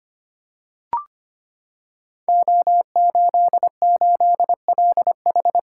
Morse code, heard as a keyed tone, about 25 wpm, 700 Hz. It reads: O88LH
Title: Code Groups - 5 Letter:Numbers - 25wpm